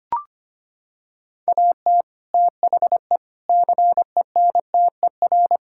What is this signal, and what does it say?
Morse code, heard as a keyed tone, about 25 wpm, 700 Hz. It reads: AT THE CENTER
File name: practice / 25wpm / Sentences with 4-Word Phrases - 25wpm